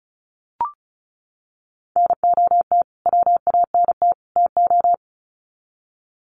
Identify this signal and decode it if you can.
Morse code, heard as a keyed tone, about 34 wpm, 700 Hz. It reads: NOT WANT TO
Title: Sets of 3 Words 35wpm